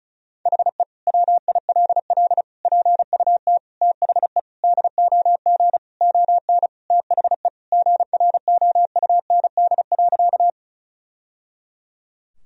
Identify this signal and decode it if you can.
Morse code, heard as a keyed tone, about 34 wpm, 700 Hz. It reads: HE WILL PUT THE DOG ON THE GROUND.
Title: Sentences from Top 1000 Words - Review Part 2 - 35wpm